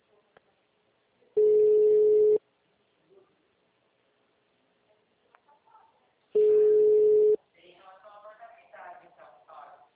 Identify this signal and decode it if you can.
Ringback tone (Europe, 425 Hz).
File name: out- -2024- - - .18905